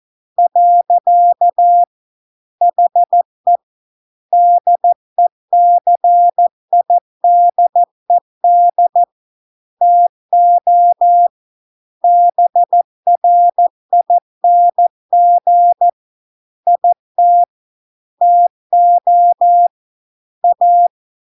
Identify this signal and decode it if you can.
Morse code, heard as a keyed tone, about 14 wpm, 700 Hz. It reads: . HE DECIDED TO BRING IT TO A